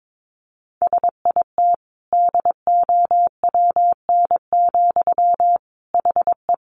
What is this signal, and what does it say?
Morse code, heard as a keyed tone, about 22 wpm, 700 Hz. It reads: SIT DOWN, HE
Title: Sentences from Top 1500 Words 22wpm